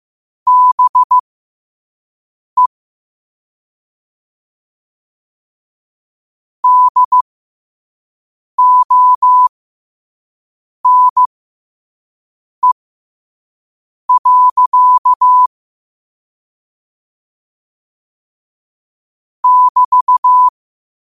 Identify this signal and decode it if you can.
Morse code, heard as a keyed tone, about 15 wpm, spaced as for 5 wpm, 1000 Hz. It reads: BE DONE. <BT>
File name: qotdfiles / 5wpm